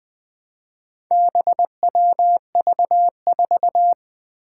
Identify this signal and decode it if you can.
Morse code, heard as a keyed tone, about 21 wpm, 700 Hz. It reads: BWV4
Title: Four Challenging Letter-Number Combinations 20wpm